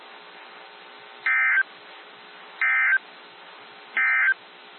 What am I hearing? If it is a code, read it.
Emergency Alert System (SAME) tones, heard as harsh buzzing bursts. It is heard from 1.1 to 4.4 s.
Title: NOAA Weather Radio - WXJ73